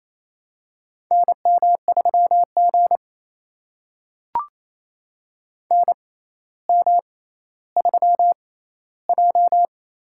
Morse code, heard as a keyed tone, about 29 wpm, spaced as for 14 wpm, 700 Hz. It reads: NM3G NM3J
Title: USA Call Signs - 4 Characters - 6x Character Spacing - 28wpm